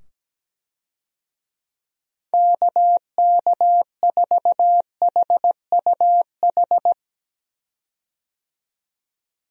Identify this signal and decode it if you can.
Morse code, heard as a keyed tone, about 17 wpm, 700 Hz. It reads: KK4HUH
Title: USA Call Signs - 6 Characters - 17wpm